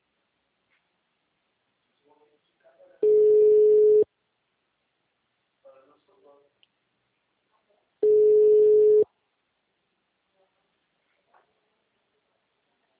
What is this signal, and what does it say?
Ringback tone (Europe, 425 Hz), with speech.